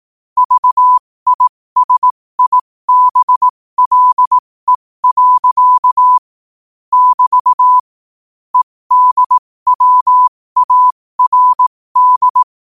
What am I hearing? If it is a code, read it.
Morse code, heard as a keyed tone, about 18 wpm, spaced as for 15 wpm, 1000 Hz. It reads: VISIBLE. <BT> EDWARD
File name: qotdfiles / 15wpm